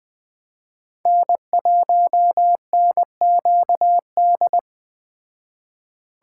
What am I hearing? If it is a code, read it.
Morse code, heard as a keyed tone, about 21 wpm, 700 Hz. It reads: N1NQD